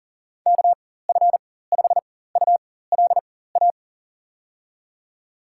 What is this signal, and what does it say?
Morse code, heard as a keyed tone, about 40 wpm, spaced as for 19 wpm, 700 Hz. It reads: KF5ULA
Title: USA Call Signs - 6 Characters - 4x Character Spacing - 40wpm